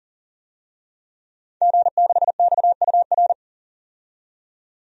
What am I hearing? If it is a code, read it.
Morse code, heard as a keyed tone, about 37 wpm, 700 Hz. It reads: G6XUR
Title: DXCC-100 Call Signs - 5 Characters - 40wpm